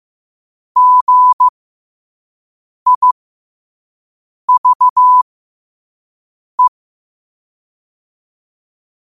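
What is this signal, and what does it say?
Morse code, heard as a keyed tone, about 15 wpm, spaced as for 5 wpm, 1000 Hz. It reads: GIVE
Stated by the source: Morse code quote of the day sent at 5 wpm.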